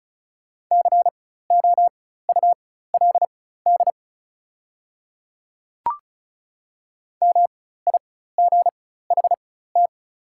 Morse code, heard as a keyed tone, about 36 wpm, spaced as for 16 wpm, 690 Hz. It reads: COULD MIGHT
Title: 4x Character Spacing - Top 500 Words - 35wpm